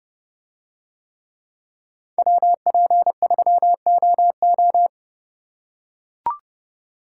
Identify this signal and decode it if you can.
Morse code, heard as a keyed tone, about 29 wpm, 700 Hz. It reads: WP3OO